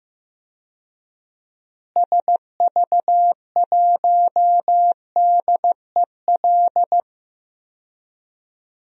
Morse code, heard as a keyed tone, about 15 wpm, 700 Hz. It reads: SV1DEL
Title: DXCC-100 Call Signs - 6 Characters - Repeated 2x - 15wpm